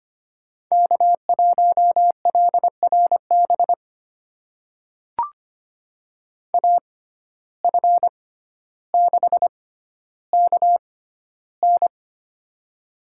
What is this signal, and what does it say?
Morse code, heard as a keyed tone, about 25 wpm, spaced as for 13 wpm, 700 Hz. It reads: K1LRB AF6KN
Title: USA Call Signs - 5 Characters - 6x Character Spacing - 25wpm